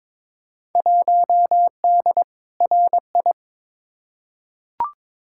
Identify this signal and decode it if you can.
Morse code, heard as a keyed tone, about 21 wpm, 700 Hz. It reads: 1D RI
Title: Field Day Exchanges - 22wpm